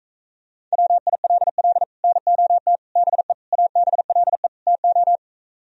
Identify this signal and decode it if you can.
Morse code, heard as a keyed tone, about 45 wpm, 700 Hz. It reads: WILL NOT BE ABLE TO